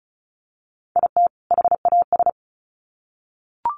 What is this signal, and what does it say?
Morse code, heard as a keyed tone, about 34 wpm, 700 Hz. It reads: IT HAS
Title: Sets of 2 Words 35wpm